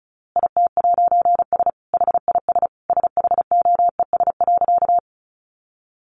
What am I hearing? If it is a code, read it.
Morse code, heard as a keyed tone, about 34 wpm, 700 Hz. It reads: IT'S HIS SHOES.
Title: A Sound Of Thunder 35wpm